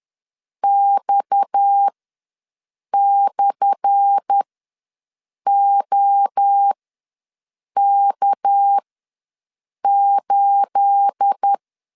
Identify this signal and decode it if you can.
Morse code, heard as a keyed tone, about 10 wpm, spaced as for 6 wpm, 800 Hz. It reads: X/OK8
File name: LEZIONI / 10 SIGNS / 10C DOT ws 10wpm 4213